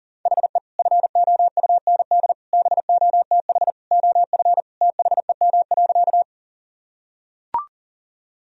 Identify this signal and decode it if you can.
Morse code, heard as a keyed tone, about 40 wpm, 700 Hz. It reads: HE FOUND BOTH OF THEM.
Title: Sentences from Top 1000 Words - Review Part 1 - 40wpm